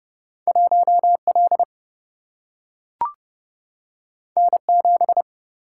Morse code, heard as a keyed tone, about 30 wpm, 700 Hz. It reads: 1 L N 7
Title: Single-Letter Single-Number 30wpm